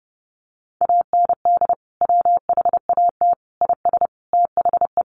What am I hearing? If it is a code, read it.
Morse code, heard as a keyed tone, about 30 wpm, 700 Hz. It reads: AND WHAT IS THE
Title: Sentences from Top 1500 Words 30wpm